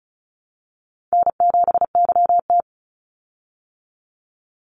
Morse code, heard as a keyed tone, about 34 wpm, 700 Hz. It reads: N7YT
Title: Call Signs Repeated 3x - 35wpm